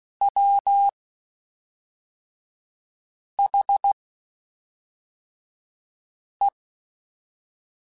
Morse code, heard as a keyed tone, about 16 wpm, 800 Hz. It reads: W H E
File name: Spaced / Chuckles Spaced-mp3s / Temptation sp @5